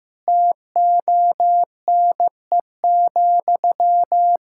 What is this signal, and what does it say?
Morse code, heard as a keyed tone, about 15 wpm, 700 Hz. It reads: TONE,
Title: Sentences from Top 1600 Words 15wpm